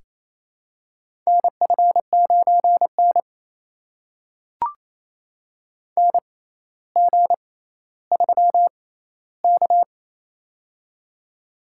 Morse code, heard as a keyed tone, about 28 wpm, spaced as for 14 wpm, 700 Hz. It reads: NF9N NG3K
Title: USA Call Signs - 4 Characters - 6x Character Spacing - 28wpm